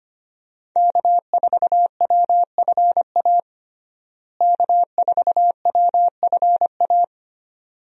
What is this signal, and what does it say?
Morse code, heard as a keyed tone, about 25 wpm, 700 Hz. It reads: K4WFA K4WFA
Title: USA Call Signs - 5 Characters - Repeated 2x - 25wpm